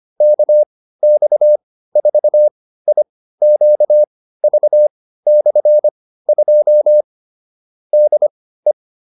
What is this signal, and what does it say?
Morse code, heard as a keyed tone, about 25 wpm, spaced as for 15 wpm, 600 Hz. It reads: KX4IQV/2 DE
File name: QSO308 15